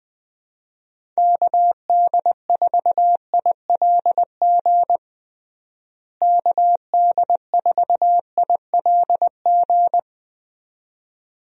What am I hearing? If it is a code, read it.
Morse code, heard as a keyed tone, about 20 wpm, 700 Hz. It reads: KD4ILG KD4ILG
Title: USA Call Signs - 6 Characters - Repeated 2x - 20wpm